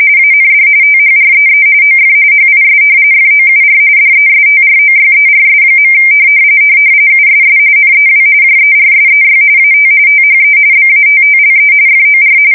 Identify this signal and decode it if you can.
RTTY, about 45.45 baud. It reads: AND THREE OF HIS FOUR DESTROYERS. USS NEW JERSEY AT THE BATTLE FOR
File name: fsk170 / battleshipnj